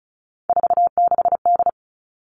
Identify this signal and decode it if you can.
Morse code, heard as a keyed tone, about 34 wpm, 700 Hz. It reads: 46D
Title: All Cut Numbers - 3-Digit 35wpm